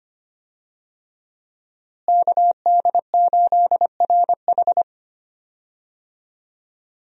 Morse code, heard as a keyed tone, about 25 wpm, 700 Hz. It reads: KD8RH
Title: USA Call Signs - 5 Characters - Repeated 2x - 25wpm